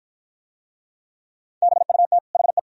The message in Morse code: BUT HE

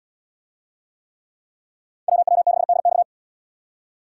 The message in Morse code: WW6DX

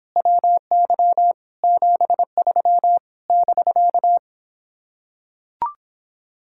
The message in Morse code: WY 73 <BK>

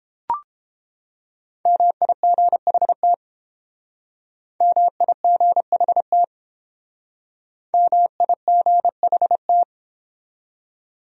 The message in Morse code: MIGHT MIGHT MIGHT

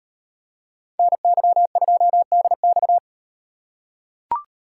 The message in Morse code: NY2DX

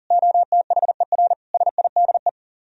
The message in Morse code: OTHER SIDE

